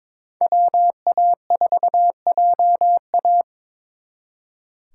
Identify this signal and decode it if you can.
Morse code, heard as a keyed tone, about 27 wpm, 700 Hz. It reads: WA4JA
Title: Speed Racing - USA Call Signs 22wpm